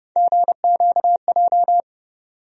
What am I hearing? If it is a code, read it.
Morse code, heard as a keyed tone, about 30 wpm, 700 Hz. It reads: GQJ
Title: USA Call Sign Postfixes - 3 Letters 30wpm